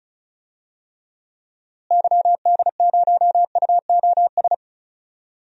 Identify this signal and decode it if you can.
Morse code, heard as a keyed tone, about 36 wpm, 700 Hz. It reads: YD0UOS
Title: DXCC-100 Call Signs - 6 Characters - Repeated 2x - 35wpm